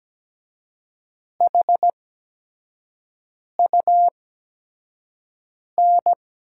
Morse code, heard as a keyed tone, about 17 wpm, spaced as for 5 wpm, 700 Hz. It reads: HUN
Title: USA Call Signs - 6 Characters - 8x Character Spacing - 17wpm